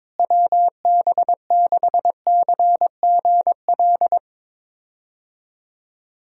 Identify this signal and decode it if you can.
Morse code, heard as a keyed tone, about 21 wpm, 700 Hz. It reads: WB6CGL
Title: USA Call Signs - 6 Characters - Repeated 2x - 22wpm